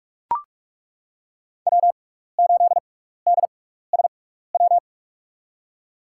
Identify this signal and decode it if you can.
Morse code, heard as a keyed tone, about 42 wpm, spaced as for 15 wpm, 700 Hz. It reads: W8DSW
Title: USA Call Signs - 5 Characters - 6x Character Spacing - 45wpm